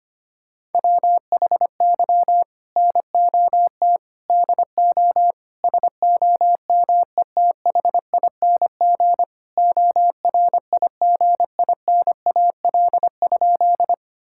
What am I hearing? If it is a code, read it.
Morse code, heard as a keyed tone, about 25 wpm, 700 Hz. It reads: WHY NOT DO SOMETHING ORIGINAL?
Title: Sentences from Top 1000 Words - Review Part 6 - 25wpm